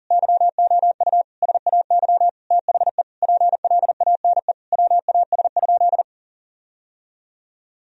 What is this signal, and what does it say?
Morse code, heard as a keyed tone, about 40 wpm, 700 Hz. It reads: YOU SAY THE PLANE WAS?
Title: Sentences from Top 1000 Words - Review Part 3 - 40wpm